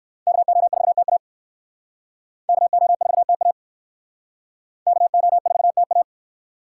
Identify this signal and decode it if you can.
Morse code, heard as a keyed tone, about 57 wpm, 700 Hz. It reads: KO4TA KO4TA KO4TA